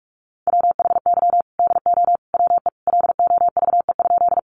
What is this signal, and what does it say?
Morse code, heard as a keyed tone, about 44 wpm, 700 Hz. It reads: WHY DO WE LOVE?
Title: Sentences from Top 400 Words 45wpm